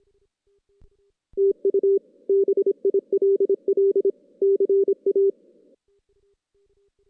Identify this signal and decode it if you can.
Morse code, heard as a keyed tone, about 25 wpm, 390 Hz. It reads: TU BILL CA
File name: 2016 JAN CW NAQP / 2016 Jan naqp k5ot - 0-6